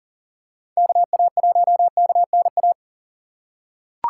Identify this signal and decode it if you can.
Morse code, heard as a keyed tone, about 40 wpm, 700 Hz. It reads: KA1KNA
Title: USA Call Signs - 6 Characters - 6x Character Spacing - 40wpm